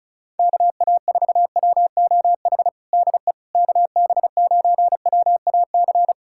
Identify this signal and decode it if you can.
Morse code, heard as a keyed tone, about 36 wpm, 700 Hz. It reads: KA4WOH DE KB9WAC
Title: USA Callsign DE Callsign 35wpm